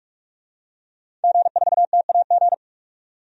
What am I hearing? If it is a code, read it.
Morse code, heard as a keyed tone, about 45 wpm, 700 Hz. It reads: G4TAG
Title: DXCC-100 Call Signs - 5 Characters - 45wpm